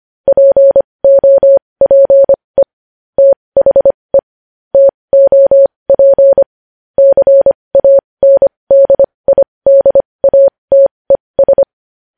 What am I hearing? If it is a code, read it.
Morse code, heard as a keyed tone, about 25 wpm, spaced as for 20 wpm, 550 Hz. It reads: POPE THE TOP CANDIDATES
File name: podcast / News headlines-1x-25x20